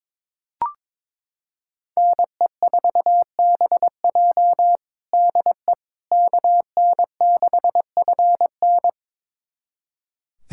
Morse code, heard as a keyed tone, about 22 wpm, 700 Hz. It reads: NE4BJ DE KN6FN
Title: USA Callsign DE Callsign - 5 Characters - 22wpm